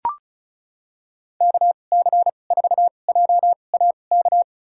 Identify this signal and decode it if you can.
Morse code, heard as a keyed tone, about 34 wpm, spaced as for 25 wpm, 700 Hz. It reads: KC4JAK
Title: USA Call Signs - 6 Characters - 2x Character Spacing - 35wpm